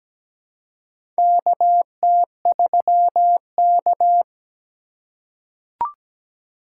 Morse code, heard as a keyed tone, about 17 wpm, 700 Hz. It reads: KT3K
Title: USA Call Signs - 4 Characters - 6x Character Spacing - 17wpm